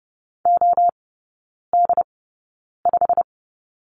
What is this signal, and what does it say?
Morse code, heard as a keyed tone, about 30 wpm, spaced as for 9 wpm, 700 Hz. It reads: OD5
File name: course / 30:10wpm / 087 Lesson 15e - Number 5 - Three Characters 30wpm